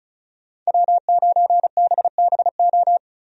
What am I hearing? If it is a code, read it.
Morse code, heard as a keyed tone, about 36 wpm, 700 Hz. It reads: W9BBO
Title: USA Call Signs - 5 Characters - 35wpm